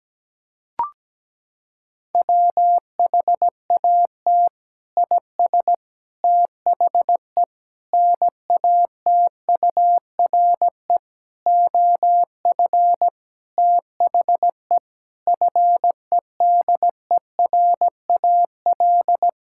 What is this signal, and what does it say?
Morse code, heard as a keyed tone, about 17 wpm, 700 Hz. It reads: WHAT IS THE NATURE OF THE FEDERAL
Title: Sentences from Top 1000 Words - Review Part 5 - 17wpm